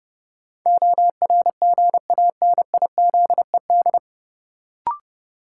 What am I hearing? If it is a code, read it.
Morse code, heard as a keyed tone, about 30 wpm, 700 Hz. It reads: ORGANIZED
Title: Sentences from Top 1300 Words - New Word Drill - 30wpm